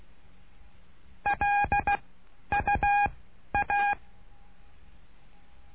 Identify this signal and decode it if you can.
Morse code, heard as a keyed tone, about 16 wpm, spaced as for 10 wpm, 820 Hz. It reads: LUA
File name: cursos / CURSO DE TELEGRAFIA / Telegrafia - Trilha 09